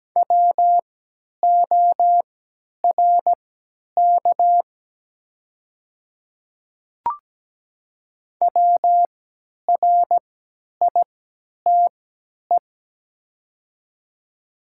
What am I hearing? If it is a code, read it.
Morse code, heard as a keyed tone, about 17 wpm, spaced as for 10 wpm, 700 Hz. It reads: WORK WRITE
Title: 3x Character Spacing - Top 200 Words - 17wpm